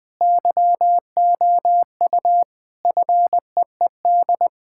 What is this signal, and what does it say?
Morse code, heard as a keyed tone, about 20 wpm, 700 Hz. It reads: YOU FEED
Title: Sentences from Top 1600 Words 20wpm